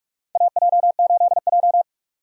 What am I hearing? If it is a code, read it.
Morse code, heard as a keyed tone, about 46 wpm, 700 Hz. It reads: AJ8J
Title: USA Call Signs - 4 Characters - 45wpm